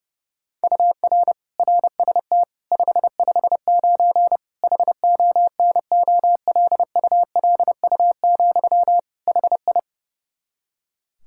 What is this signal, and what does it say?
Morse code, heard as a keyed tone, about 30 wpm, 700 Hz. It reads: UR RST 559 HONOLULU, HI
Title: Practice QSO - Exchange 1 - Beginning - 30wpm